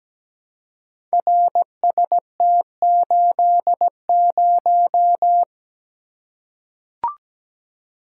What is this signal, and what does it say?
Morse code, heard as a keyed tone, about 17 wpm, 700 Hz. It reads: RST80
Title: Code Groups - 5 Letter:Numbers - 17wpm